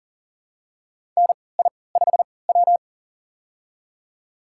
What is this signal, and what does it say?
Morse code, heard as a keyed tone, about 40 wpm, spaced as for 23 wpm, 700 Hz. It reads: NI5W
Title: USA Call Signs - 4 Characters - 3x Character Spacing - 40wpm